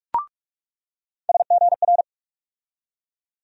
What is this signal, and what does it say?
Morse code, heard as a keyed tone, about 42 wpm, 700 Hz. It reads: SGR
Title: USA Call Sign Postfixes - 3 Letters 45wpm